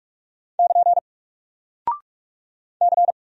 Morse code, heard as a keyed tone, about 44 wpm, 700 Hz. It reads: <KN> C
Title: Single Character (Letter:Number:Punctuation:Prosign) 45wpm